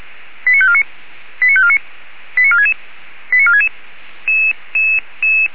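Five-tone selective call (ZVEI), heard as quick runs of five tones, tones of about 70 ms. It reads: 89549 89549 89466 89466
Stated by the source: ZVEI five-tone call received by SDR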